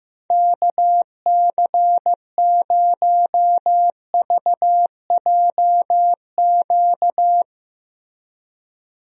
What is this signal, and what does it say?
Morse code, heard as a keyed tone, about 15 wpm, 700 Hz. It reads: KC0VJQ
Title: USA Call Signs - 6 Characters - Repeated 3x - 15wpm